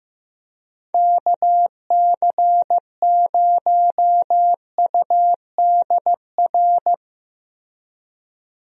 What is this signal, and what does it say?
Morse code, heard as a keyed tone, about 15 wpm, 700 Hz. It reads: KC0UDR
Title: USA Call Signs - 6 Characters - Repeated 2x - 15wpm